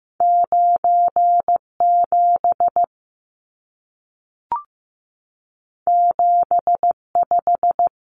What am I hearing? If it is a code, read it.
Morse code, heard as a keyed tone, about 15 wpm, 700 Hz. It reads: 9 7 7 5
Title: Numbers - 2-Digit 15wpm